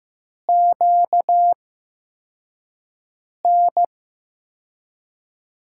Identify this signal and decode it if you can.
Morse code, heard as a keyed tone, about 15 wpm, spaced as for 4 wpm, 700 Hz. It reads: QN
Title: USA Call Signs - 5 Characters - 8x Character Spacing - 15wpm